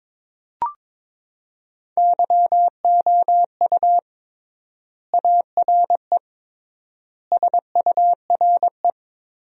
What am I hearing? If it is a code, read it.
Morse code, heard as a keyed tone, about 21 wpm, 700 Hz. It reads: YOU ARE SURE
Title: 3x Word-Spacing - Sentences from Top 300 Words - 22wpm